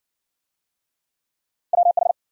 Morse code, heard as a keyed tone, about 53 wpm, 700 Hz. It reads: WH